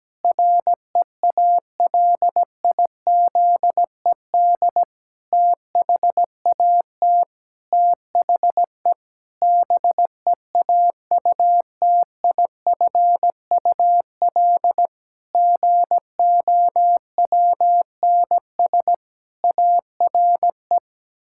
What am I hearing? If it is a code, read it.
Morse code, heard as a keyed tone, about 17 wpm, 700 Hz. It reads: REALIZED THAT THE BEAUTIFUL GOWNS ARE
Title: Pangram Challenge v2 - 17wpm